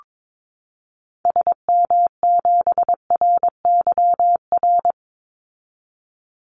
Morse code, heard as a keyed tone, about 22 wpm, 700 Hz. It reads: SM7RYR
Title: Call Signs Repeated 2x - 22wpm